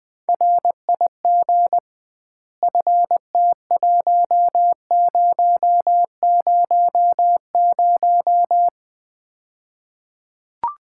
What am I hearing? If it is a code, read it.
Morse code, heard as a keyed tone, about 20 wpm, 700 Hz. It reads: RIG FT1000
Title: Common QSO Phrases - 2x Word-Spacing - 20wpm